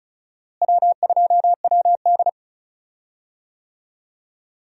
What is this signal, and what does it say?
Morse code, heard as a keyed tone, about 36 wpm, 700 Hz. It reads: W2WD
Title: USA Call Signs - 4 Characters - Repeated 2x - 35wpm